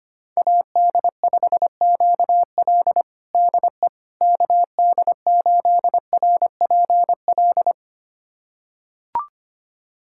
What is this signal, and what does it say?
Morse code, heard as a keyed tone, about 25 wpm, 700 Hz. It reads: AD5QL DE KD8RPL
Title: USA Callsign DE Callsign 25wpm